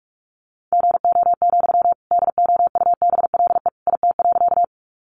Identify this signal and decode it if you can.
Morse code, heard as a keyed tone, about 44 wpm, 700 Hz. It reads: GO, DOUBLE IT.
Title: Sentences from Top 1400 Words 45wpm